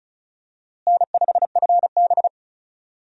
Morse code, heard as a keyed tone, about 36 wpm, 700 Hz. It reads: N5FB
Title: USA Call Signs - 4 Characters - Repeated 2x - 35wpm